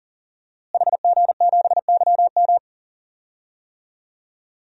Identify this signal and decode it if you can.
Morse code, heard as a keyed tone, about 40 wpm, 700 Hz. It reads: HG7YM
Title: DXCC-100 Call Signs - 5 Characters - Repeated 2x - 40wpm